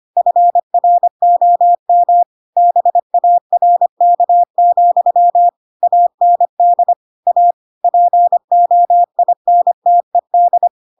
Morse code, heard as a keyed tone, about 25 wpm, 700 Hz. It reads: FROM BARK, AND A POINTED
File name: Pinocchio / Pinocchio ch8 25 25 ik1hge